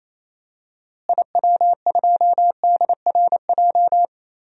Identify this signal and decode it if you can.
Morse code, heard as a keyed tone, about 28 wpm, 700 Hz. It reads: IW2DRJ
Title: DXCC-100 Call Signs - 6 Characters - Repeated 3x - 28wpm